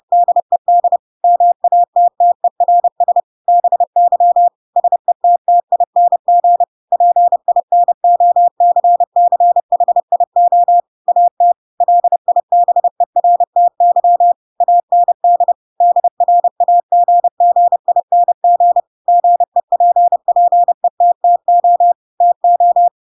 Morse code, heard as a keyed tone, about 30 wpm, 700 Hz. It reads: DED MATTERS BY SETTING PINOCCHIO AT LIBERTY AND DRAGGING GEPPETTO TO
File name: Pinocchio / Pinocchio ch3 30 30 ik1hge